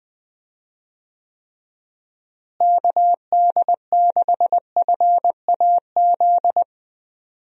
Morse code, heard as a keyed tone, about 21 wpm, 700 Hz. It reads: KD6FAZ
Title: USA Call Signs - 6 Characters - Repeated 2x - 20wpm